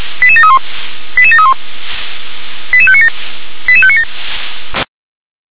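Five-tone selective call (ZVEI), heard as quick runs of five tones, tones of about 70 ms. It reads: 88731 88731 88597 88597
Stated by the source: ZVEI five-tone selective call received on an SDR.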